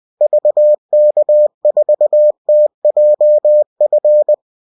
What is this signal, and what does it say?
Morse code, heard as a keyed tone, about 20 wpm, 600 Hz. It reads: VK4TJF